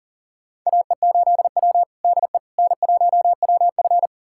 Morse code, heard as a keyed tone, about 40 wpm, 700 Hz. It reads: AE8W DE N1WF
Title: USA Callsign DE Callsign - 4 Characters - 40wpm